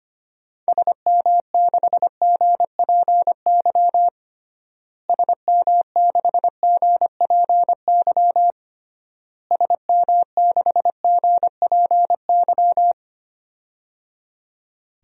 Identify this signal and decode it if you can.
Morse code, heard as a keyed tone, about 25 wpm, 700 Hz. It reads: SM6GPY SM6GPY SM6GPY
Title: DXCC-100 Call Signs - 6 Characters - Repeated 3x - 25wpm